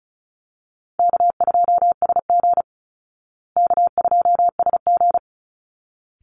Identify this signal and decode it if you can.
Morse code, heard as a keyed tone, about 34 wpm, 700 Hz. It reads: K2SG K2SG
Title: Call Signs Repeated 3x - 35wpm